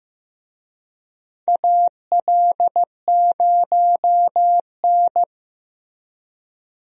Morse code, heard as a keyed tone, about 15 wpm, 700 Hz. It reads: AL0N
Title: USA Call Signs - 4 Characters - Repeated 3x - 15wpm